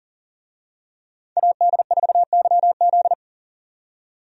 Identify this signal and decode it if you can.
Morse code, heard as a keyed tone, about 40 wpm, 700 Hz. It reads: AD4YZ
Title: USA Call Signs - 5 Characters - 40wpm